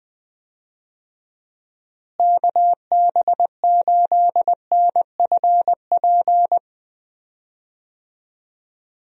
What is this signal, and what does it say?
Morse code, heard as a keyed tone, about 20 wpm, 700 Hz. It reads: KB8NFP
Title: USA Call Signs - 6 Characters - Repeated 2x - 20wpm